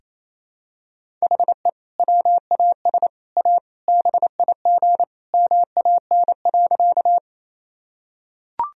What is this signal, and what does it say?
Morse code, heard as a keyed tone, about 28 wpm, 700 Hz. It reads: HE WAS A BIG MAN.
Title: Sentences from Top 1000 Words - Review Part 1 - 28wpm